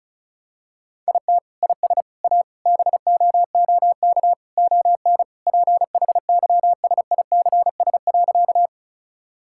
Morse code, heard as a keyed tone, about 36 wpm, 700 Hz. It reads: IT IS A BOOK ON PHYSICS.
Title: Sentences from Top 1000 Words - Review Part 5 - 35wpm